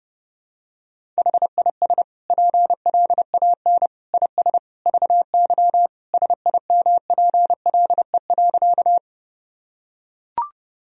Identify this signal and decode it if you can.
Morse code, heard as a keyed tone, about 30 wpm, 700 Hz. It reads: HIS PLAN IS VY SIMPLE.
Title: Sentences from Top 500 Words and Abbreviations 30wpm